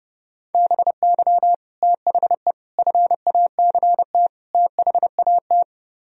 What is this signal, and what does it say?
Morse code, heard as a keyed tone, about 37 wpm, 690 Hz. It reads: BY THE FACT THAT
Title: Speed-Racing - Sets of 4 Words 30wpm